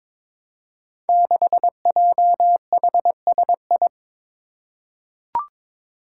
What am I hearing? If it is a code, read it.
Morse code, heard as a keyed tone, about 22 wpm, 700 Hz. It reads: 6JHSI
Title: Five Challenging Letter-Number Combinations 22wpm